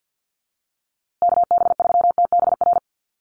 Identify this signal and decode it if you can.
Morse code, heard as a keyed tone, about 50 wpm, 700 Hz. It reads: KB3TBR
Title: USA Call Signs Repeated 3x 50wpm